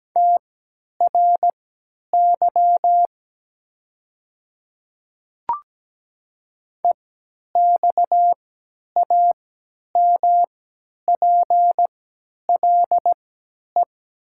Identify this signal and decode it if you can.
Morse code, heard as a keyed tone, about 17 wpm, spaced as for 10 wpm, 700 Hz. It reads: TRY EXAMPLE
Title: 3x Character Spacing - Top 300 Words - 17wpm